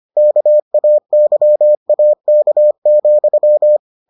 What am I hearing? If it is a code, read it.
Morse code, heard as a keyed tone, about 25 wpm, 600 Hz. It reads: KAYAK,